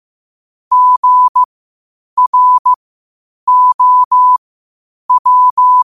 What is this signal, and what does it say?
Morse code, heard as a keyed tone, about 15 wpm, spaced as for 8 wpm, 1000 Hz. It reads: GROW
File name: qotdfiles / 8wpm